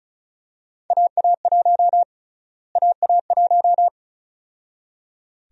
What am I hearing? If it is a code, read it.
Morse code, heard as a keyed tone, about 36 wpm, 700 Hz. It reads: AA1 AA1
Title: USA Call Sign Prefixes - Repeated 2x - 35wpm